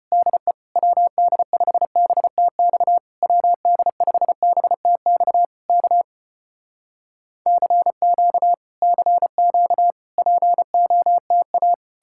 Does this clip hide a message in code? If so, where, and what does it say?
Yes: Morse code from 0.1 to 11.7 s, heard as a keyed tone, about 34 wpm, 700 Hz. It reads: DE WD5BTX WD5BTX K CQ CQ POTA